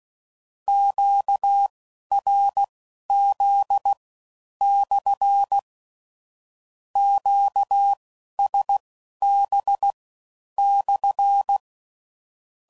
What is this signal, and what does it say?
Morse code, heard as a keyed tone, about 16 wpm, spaced as for 11 wpm, 800 Hz. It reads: QRZ/ QSB/
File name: CW Code Groups @15WPM / Q Calls / Q Calls SPSL1